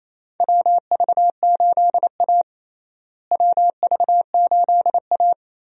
Morse code, heard as a keyed tone, about 28 wpm, 700 Hz. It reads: WV8A WV8A
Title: USA Call Signs - 4 Characters - Repeated 3x - 28wpm